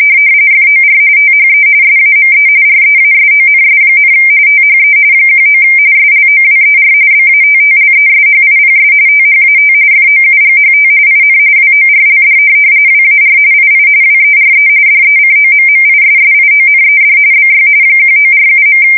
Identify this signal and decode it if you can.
RTTY, about 45.45 baud. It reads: FOR THE SAME REASONS, THE 19 SLOWER BATTLESHIPS THAT FOUGHT IN WORLD WAR II WERE DECOMMISSIONED BY 1947.
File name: fsk170 / USS-New-Jersey